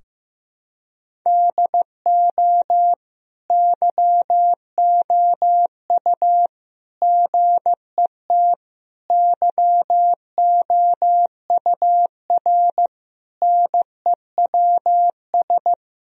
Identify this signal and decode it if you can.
Morse code, heard as a keyed tone, about 16 wpm, 700 Hz. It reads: DO YOU GET YOUR NEWS
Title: Sentences from Top 1000 Words - Review Part 4 - 15wpm